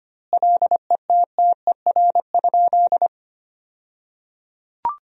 Morse code, heard as a keyed tone, about 25 wpm, 700 Hz. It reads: LETTER?
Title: 2x Word-Spacing - Sentences from Top 300 Words - 25wpm